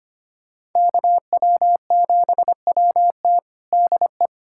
Morse code, heard as a keyed tone, about 25 wpm, 700 Hz. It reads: KW7WT DE
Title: USA Callsign DE Callsign - 5 Characters - 25wpm